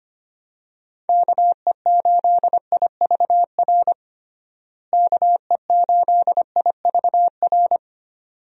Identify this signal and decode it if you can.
Morse code, heard as a keyed tone, about 25 wpm, 700 Hz. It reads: KE8IVR KE8IVR
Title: USA Call Signs - 6 Characters - Repeated 2x - 25wpm